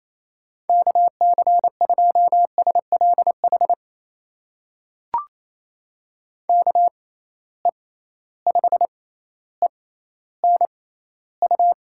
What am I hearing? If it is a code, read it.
Morse code, heard as a keyed tone, about 28 wpm, spaced as for 14 wpm, 700 Hz. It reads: KC2SLH KE5ENU
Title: USA Call Signs - 6 Characters - 6x Character Spacing - 28wpm